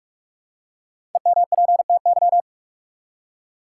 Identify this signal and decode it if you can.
Morse code, heard as a keyed tone, about 46 wpm, 700 Hz. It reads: EMPTY
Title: Five Letter Words - 45wpm